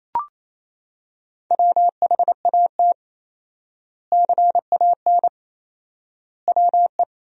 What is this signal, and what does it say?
Morse code, heard as a keyed tone, about 28 wpm, 700 Hz. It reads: WHAT CAN WE
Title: 4x Word-Spacing - Sentences from Top 100 Words - 28wpm